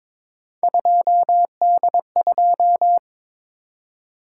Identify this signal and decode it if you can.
Morse code, heard as a keyed tone, about 22 wpm, 700 Hz. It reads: 2D2
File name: practice / 22wpm / Three Letter-Number Combinations - 22wpm